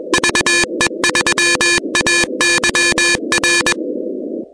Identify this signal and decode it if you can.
Morse code, heard as a keyed tone, about 21 wpm, 1750 Hz, with noise. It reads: VE3AYR
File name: contests / 2019-cw-wpx-kt5j-k5pi- -2330 0-16